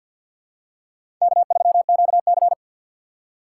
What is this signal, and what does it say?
Morse code, heard as a keyed tone, about 52 wpm, 700 Hz. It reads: K3QC
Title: USA Call Signs - 4 Characters - 3x Character Spacing - 50wpm